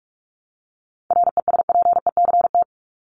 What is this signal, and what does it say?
Morse code, heard as a keyed tone, about 44 wpm, 700 Hz. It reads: RESPECT
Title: Top 1501-1750 Words 45wpm